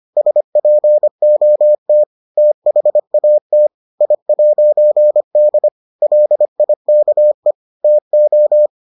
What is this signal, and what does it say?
Morse code, heard as a keyed tone, about 25 wpm, 600 Hz. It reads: SPOT THAT I'D LIKE TO